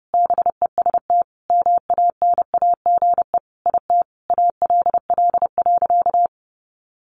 Morse code, heard as a keyed tone, about 30 wpm, 700 Hz. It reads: BEST MANAGE IT ALL.
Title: Sentences from Top 1400 Words 30wpm